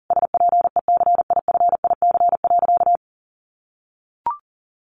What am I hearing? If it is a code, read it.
Morse code, heard as a keyed tone, about 40 wpm, 700 Hz. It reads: SPECIFIC.
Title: Sentences from Top 1300 Words 40wpm